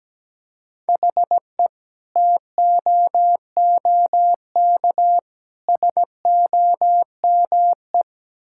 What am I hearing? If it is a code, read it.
Morse code, heard as a keyed tone, about 17 wpm, 700 Hz. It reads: HE TOOK SOME
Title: Sentences from Top 1600 Words 17wpm